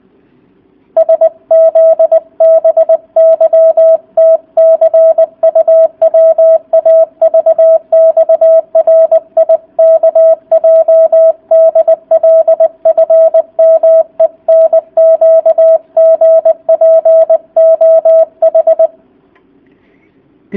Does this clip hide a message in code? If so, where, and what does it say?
Yes: Morse code from 1.0 to 18.9 s, heard as a keyed tone, about 20 wpm, 650 Hz, with radio noise. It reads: SZBYTCUWAVXRIKJDLFMENQGPOH